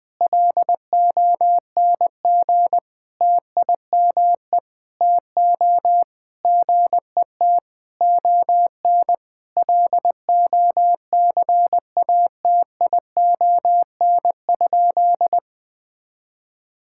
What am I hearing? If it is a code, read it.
Morse code, heard as a keyed tone, about 20 wpm, 700 Hz. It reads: LONG TIME TO GET ON LOCATION?
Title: Sentences from Top 1000 Words - Review Part 6 - 20wpm